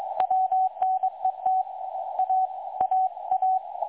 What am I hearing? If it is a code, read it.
Morse code, heard as a keyed tone, about 21 wpm, 750 Hz. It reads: WNET A AA
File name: dlazaawansowanych / nagraniazpasma / websdr recording 3565.0kHz